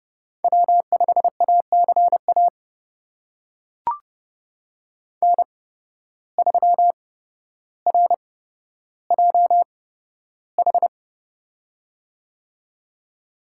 Morse code, heard as a keyed tone, about 30 wpm, spaced as for 13 wpm, 700 Hz. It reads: W5ACA N3RJH
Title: USA Call Signs - 8x Character Spacing - 30wpm